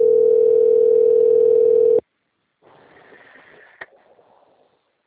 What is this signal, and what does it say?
Ringback tone (North America), 440 + 480 Hz.
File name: out- - - - - .17228